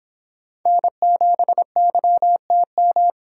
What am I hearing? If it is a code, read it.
Morse code, heard as a keyed tone, about 25 wpm, 700 Hz. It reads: N7YTM